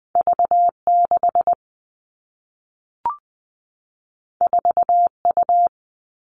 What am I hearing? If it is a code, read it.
Morse code, heard as a keyed tone, about 20 wpm, 700 Hz. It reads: V6 4U
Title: All Cut Numbers - 2-Digit 20wpm